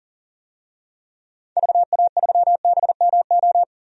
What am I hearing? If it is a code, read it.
Morse code, heard as a keyed tone, about 39 wpm, 700 Hz. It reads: VA3BMO
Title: DXCC-100 Call Signs - 6 Characters - 40wpm